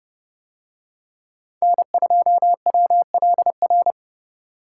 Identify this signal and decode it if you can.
Morse code, heard as a keyed tone, about 29 wpm, 700 Hz. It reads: N2WLR